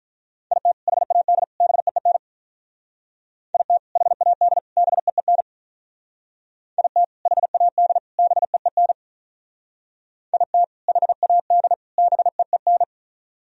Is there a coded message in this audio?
Yes: Morse code from 0.5 to 12.8 s, heard as a keyed tone, about 47 wpm, 700 Hz. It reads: IT HAD BEEN IT HAD BEEN IT HAD BEEN IT HAD BEEN